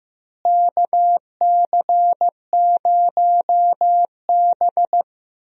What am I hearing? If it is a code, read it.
Morse code, heard as a keyed tone, about 15 wpm, 700 Hz. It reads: KC0B
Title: USA Call Signs - 4 Characters - Repeated 2x - 15wpm